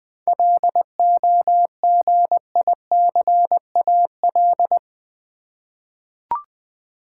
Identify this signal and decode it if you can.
Morse code, heard as a keyed tone, about 20 wpm, 700 Hz. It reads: LOGICAL
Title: Seven Letter Words - 20wpm